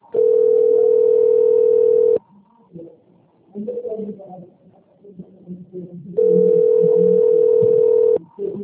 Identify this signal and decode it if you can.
Ringback tone (North America), 440 + 480 Hz, with speech.
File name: rg-1600-unknown- - - .823